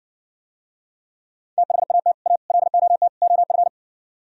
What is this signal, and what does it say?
Morse code, heard as a keyed tone, about 60 wpm, 700 Hz. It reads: THAT A LOT OF